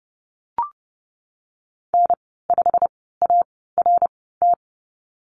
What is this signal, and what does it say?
Morse code, heard as a keyed tone, about 30 wpm, spaced as for 17 wpm, 700 Hz. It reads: N5ART
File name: Lesson 19g - Letter w - Call Signs 30wpm